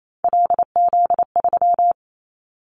Morse code, heard as a keyed tone, about 28 wpm, 700 Hz. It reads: LZ3
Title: Callsign Prefixes for Top 100 DXCC - 28wpm